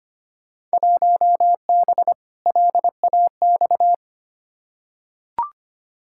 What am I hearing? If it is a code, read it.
Morse code, heard as a keyed tone, about 25 wpm, 700 Hz. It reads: 1B LAX
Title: Field Day Exchanges - 25wpm